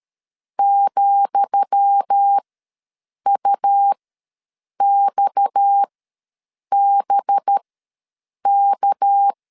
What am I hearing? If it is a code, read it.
Morse code, heard as a keyed tone, about 13 wpm, spaced as for 7 wpm, 800 Hz. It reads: ,UXBK
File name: LEZIONI / 10 SIGNS / 10D COMA ws 12wpm 3618